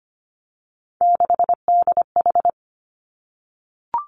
Morse code, heard as a keyed tone, about 25 wpm, 700 Hz. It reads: 6DH
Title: Three Challenging Letter-Number Combinations 25wpm